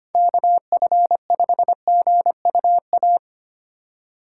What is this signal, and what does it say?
Morse code, heard as a keyed tone, about 28 wpm, 700 Hz. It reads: KF5GUA
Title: Speed Racing - USA Call Signs 25wpm